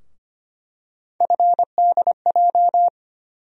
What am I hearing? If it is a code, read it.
Morse code, heard as a keyed tone, about 25 wpm, 700 Hz. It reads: FDJ
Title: USA Call Sign Postfixes - 3 Letters 25wpm